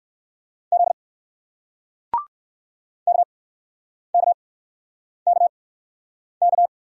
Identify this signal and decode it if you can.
Morse code, heard as a keyed tone, about 55 wpm, spaced as for 9 wpm, 700 Hz. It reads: Z KKKK
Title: Speed-Racing Single Letter-Number 45wpm